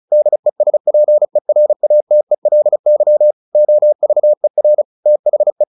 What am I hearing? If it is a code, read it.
Morse code, heard as a keyed tone, about 34 wpm, 600 Hz. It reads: DESPERATELY OVER THE
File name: the old man and the sea / the old man and the sea 1 35wpm